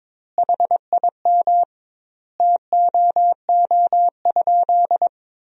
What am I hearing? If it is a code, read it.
Morse code, heard as a keyed tone, about 22 wpm, 700 Hz. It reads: HIM TOO?
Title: 2x Word-Spacing - Sentences from Top 500 Words - 22wpm